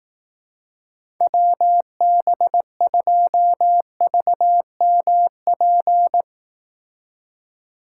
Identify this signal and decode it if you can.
Morse code, heard as a keyed tone, about 18 wpm, 700 Hz. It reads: WB2VMP
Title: Speed Racing - USA Call Signs 15wpm